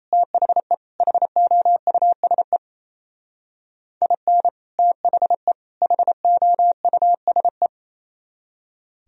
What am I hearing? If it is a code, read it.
Morse code, heard as a keyed tone, about 36 wpm, 700 Hz. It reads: THE HOUSE IN THE HOUSE